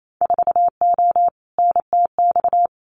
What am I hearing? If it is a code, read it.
Morse code, heard as a keyed tone, about 28 wpm, 700 Hz. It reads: 4O NTX
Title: Winter Field Day Exchanges 28wpm